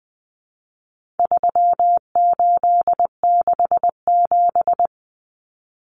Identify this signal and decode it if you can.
Morse code, heard as a keyed tone, about 20 wpm, 700 Hz. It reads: 3867
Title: Numbers - 4-Digit 20wpm